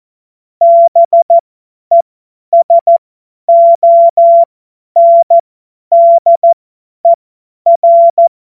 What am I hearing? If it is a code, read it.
Morse code, heard as a keyed tone, about 14 wpm, spaced as for 10 wpm, 680 Hz. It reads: BESONDER
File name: Morsekurs / Witze / Witz 09. November 2021, 50 F70